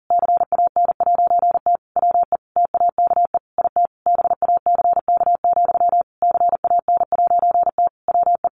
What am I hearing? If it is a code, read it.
Morse code, heard as a keyed tone, about 40 wpm, 700 Hz. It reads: CAN'T WE TAKE IT BACK, CAN'T WE
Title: A Sound Of Thunder 40wpm